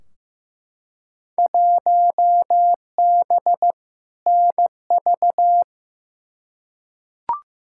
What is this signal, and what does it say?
Morse code, heard as a keyed tone, about 15 wpm, 700 Hz. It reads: 1B NV
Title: Field Day Exchanges - 15wpm